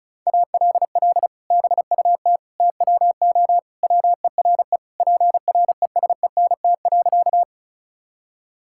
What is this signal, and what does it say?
Morse code, heard as a keyed tone, about 36 wpm, 700 Hz. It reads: ALL BUT TWO WERE PRESENT.
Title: Sentences from Top 1000 Words - Review Part 5 - 35wpm